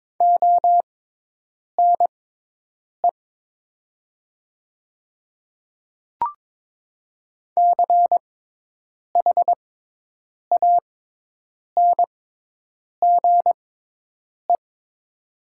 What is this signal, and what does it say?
Morse code, heard as a keyed tone, about 21 wpm, spaced as for 8 wpm, 700 Hz. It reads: ONE CHANGE
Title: 6x Character Spacing - Top 300 Words - 22wpm